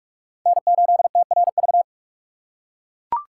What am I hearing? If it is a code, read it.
Morse code, heard as a keyed tone, about 43 wpm, 700 Hz. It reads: N8TRV
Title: USA Call Signs - 5 Characters - 3x Character Spacing - 45wpm